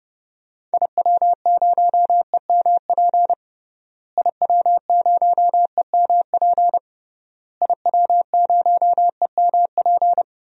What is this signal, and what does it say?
Morse code, heard as a keyed tone, about 30 wpm, 700 Hz. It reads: IW0EMP IW0EMP IW0EMP
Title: DXCC-100 Call Signs - 6 Characters - Repeated 3x - 30wpm